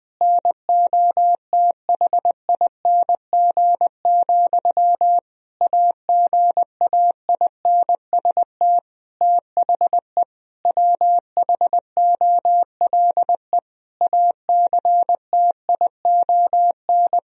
Morse code, heard as a keyed tone, about 21 wpm, 700 Hz. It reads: NOTHING, AGAINST THE WHOLE ACTION
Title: Sentences from Top 1000 Words - Review Part 4 - 20wpm